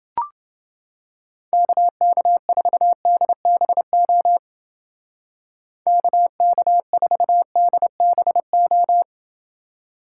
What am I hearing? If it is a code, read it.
Morse code, heard as a keyed tone, about 28 wpm, 700 Hz. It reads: KK4DBO KK4DBO